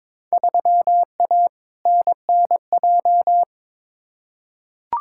Morse code, heard as a keyed tone, about 22 wpm, 700 Hz. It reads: 3A NNJ
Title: Field Day Exchanges - 22wpm